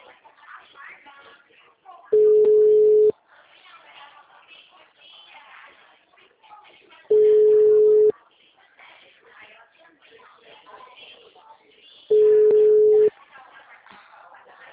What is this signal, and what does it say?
Ringback tone (Europe, 425 Hz), with speech.
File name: out- -2002- - - .6